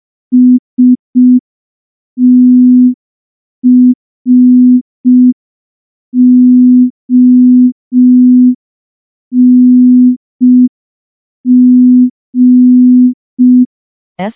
Morse code, heard as a keyed tone, about 5 wpm, 250 Hz. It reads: STRONG
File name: morse / morse-training-05WPM